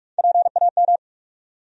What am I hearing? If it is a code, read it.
Morse code, heard as a keyed tone, about 46 wpm, 700 Hz. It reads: PAM
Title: USA Common Names - Max 5 Letters - 45wpm